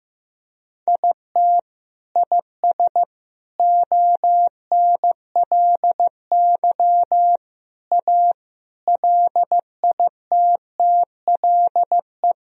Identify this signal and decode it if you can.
Morse code, heard as a keyed tone, about 16 wpm, 700 Hz. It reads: IT IS ONLY A LITTLE
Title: Sentences from Top 1000 Words - Review Part 2 - 15wpm